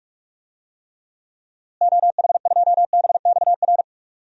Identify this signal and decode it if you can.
Morse code, heard as a keyed tone, about 42 wpm, 700 Hz. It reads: OH2BXR
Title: DXCC-100 Call Signs - 6 Characters - Repeated 2x - 45wpm